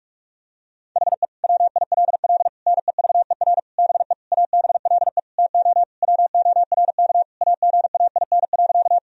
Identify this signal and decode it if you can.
Morse code, heard as a keyed tone, about 41 wpm, 700 Hz. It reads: HE WILL NEVER BE ABLE TO WORK AGAIN.
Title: Sentences from Top 1000 Words - Review Part 2 - 45wpm